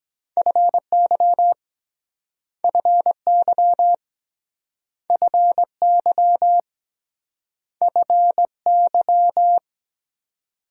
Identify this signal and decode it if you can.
Morse code, heard as a keyed tone, about 23 wpm, 700 Hz. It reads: FY FY FY FY